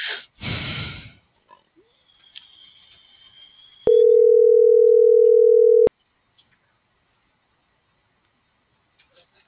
Ringback tone (North America), 440 + 480 Hz.